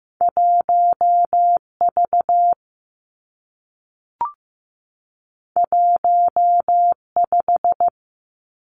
Morse code, heard as a keyed tone, about 15 wpm, 700 Hz. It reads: 1 V 1 5
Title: Two Challenging Letter-Number Combinations 15wpm